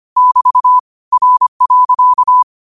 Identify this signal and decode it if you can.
Morse code, heard as a keyed tone, about 25 wpm, 1000 Hz. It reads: <BT> R.